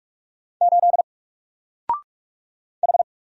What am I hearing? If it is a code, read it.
Morse code, heard as a keyed tone, about 44 wpm, 700 Hz. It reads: 8 H
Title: Single Letter-Number 45wpm